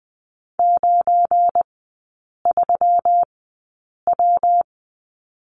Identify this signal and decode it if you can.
Morse code, heard as a keyed tone, about 20 wpm, spaced as for 8 wpm, 700 Hz. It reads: 93W
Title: 172 Lesson 28e - Number 9 - Three Characters 20wpm